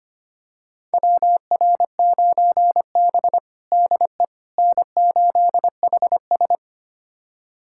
Morse code, heard as a keyed tone, about 25 wpm, 700 Hz. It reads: WR9B DE N8HS
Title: USA Callsign DE Callsign - 4 Characters - 25wpm